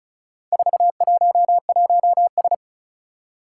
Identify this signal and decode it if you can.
Morse code, heard as a keyed tone, about 36 wpm, 700 Hz. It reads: 411S
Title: Four Challenging Letter-Number Combinations 35wpm